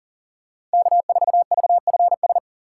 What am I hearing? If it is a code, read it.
Morse code, heard as a keyed tone, about 38 wpm, 700 Hz. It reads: K4VFS
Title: USA Call Signs - 5 Characters - 4x Character Spacing - 40wpm